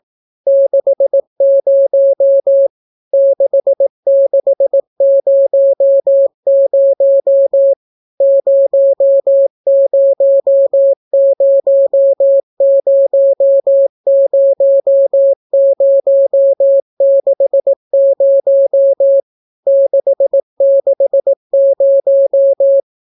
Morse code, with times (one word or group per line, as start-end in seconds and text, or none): 0.5-2.7 60
3.1-7.7 6600
8.2-19.2 00000060
19.7-22.8 660